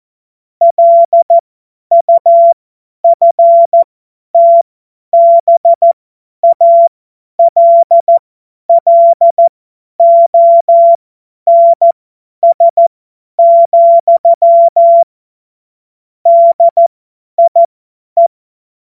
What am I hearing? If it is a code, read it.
Morse code, heard as a keyed tone, about 14 wpm, spaced as for 10 wpm, 680 Hz. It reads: LUFTBALLONS, DIE
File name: Morsekurs / Witze / Witz 02. Februar 2021, 50 F70